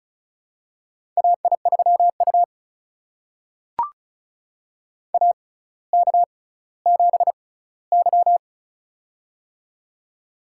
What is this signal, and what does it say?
Morse code, heard as a keyed tone, about 36 wpm, spaced as for 18 wpm, 700 Hz. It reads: AI3U AK7Y
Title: USA Call Signs - 4 Characters - 6x Character Spacing - 35wpm